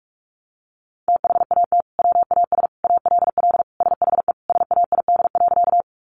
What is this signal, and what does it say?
Morse code, heard as a keyed tone, about 44 wpm, 700 Hz. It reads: THAT WAS ALL SHE SAID.
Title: Sentences from Top 100 Words 45wpm